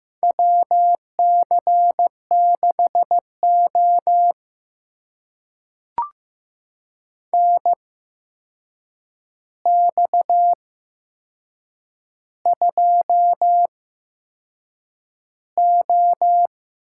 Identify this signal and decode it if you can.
Morse code, heard as a keyed tone, about 15 wpm, spaced as for 6 wpm, 700 Hz. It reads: WC6O NX2O
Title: USA Call Signs - 4 Characters - 8x Character Spacing - 15wpm